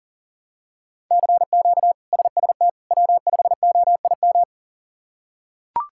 Morse code, heard as a keyed tone, about 40 wpm, 700 Hz. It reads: CQ SST W5OIM
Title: K1USN SST QSO Practice - 40wpm